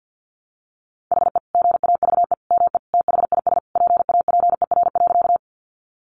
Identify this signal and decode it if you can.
Morse code, heard as a keyed tone, about 50 wpm, 700 Hz. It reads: HE GAVE ME THIS PAPER.
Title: Sentences from Top 500 Words 50wpm